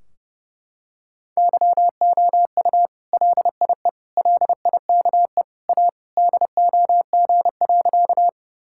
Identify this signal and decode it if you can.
Morse code, heard as a keyed tone, about 32 wpm, 700 Hz. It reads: YOU LIE LIKE A DOG.
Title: Sentences from Top 1600 Words 30wpm